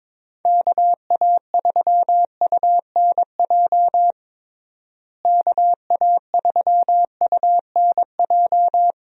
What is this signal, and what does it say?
Morse code, heard as a keyed tone, about 22 wpm, 700 Hz. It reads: KA3UNJ KA3UNJ
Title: USA Call Signs - 6 Characters - Repeated 3x - 22wpm